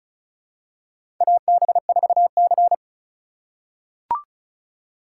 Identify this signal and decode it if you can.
Morse code, heard as a keyed tone, about 35 wpm, 700 Hz. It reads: AB4C
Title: USA Call Signs - 4 Characters - 3x Character Spacing - 35wpm